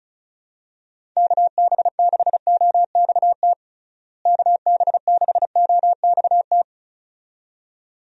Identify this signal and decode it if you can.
Morse code, heard as a keyed tone, about 36 wpm, 700 Hz. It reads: KB6OXT KB6OXT
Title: USA Call Signs - 6 Characters - Repeated 2x - 35wpm